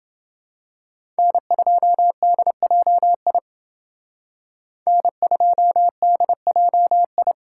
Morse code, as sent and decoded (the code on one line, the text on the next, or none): -. ..--- -.. .--- .. / -. ..--- -.. .--- ..
N2DJI N2DJI